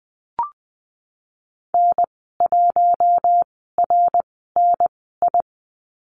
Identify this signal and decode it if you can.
Morse code, heard as a keyed tone, about 20 wpm, spaced as for 14 wpm, 700 Hz. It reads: N1RNI
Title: 069 Lesson 12g - Letter d - Call Signs 20wpm